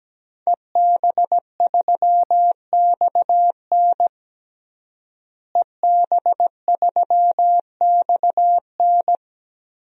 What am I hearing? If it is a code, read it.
Morse code, heard as a keyed tone, about 17 wpm, 700 Hz. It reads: EB3XN EB3XN
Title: DXCC-100 Call Signs - 5 Characters - Repeated 3x - 17wpm